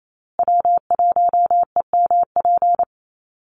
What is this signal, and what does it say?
Morse code, heard as a keyed tone, about 28 wpm, 700 Hz. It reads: W1EMP
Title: USA Call Signs Repeated 3x 28wpm